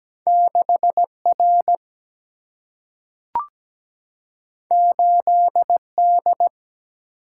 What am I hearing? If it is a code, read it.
Morse code, heard as a keyed tone, about 17 wpm, 700 Hz. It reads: 6R 8D
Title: Single-Letter Single-Number 17wpm